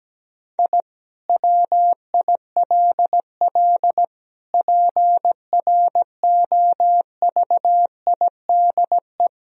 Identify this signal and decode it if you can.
Morse code, heard as a keyed tone, about 17 wpm, 700 Hz. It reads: I WILL PROVIDE
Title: Sentences from Top 1000 Words - Review Part 3 - 17wpm